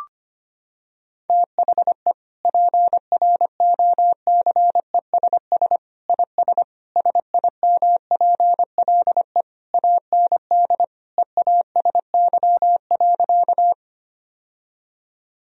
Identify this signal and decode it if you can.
Morse code, heard as a keyed tone, about 25 wpm, 700 Hz. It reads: THE PROCESS IS SIMPLE AND EASY.
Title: Sentences from Top 1000 Words - Review Part 3 - 25wpm